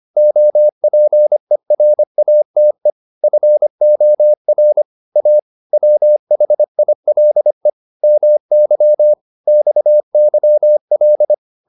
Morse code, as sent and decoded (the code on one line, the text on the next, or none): --- .--. .-. .- - . / ..-. --- .-. / .- / .-- .... .. .-.. / -- -.-- / -..- -.-- .-..
OPERATE FOR A WHILE MY XYL